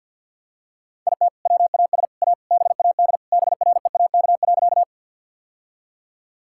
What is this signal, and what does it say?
Morse code, heard as a keyed tone, about 50 wpm, 700 Hz. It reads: IT WAS A BAD BREAK.
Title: Sentences from Top 1000 Words - Review Part 6 - 50wpm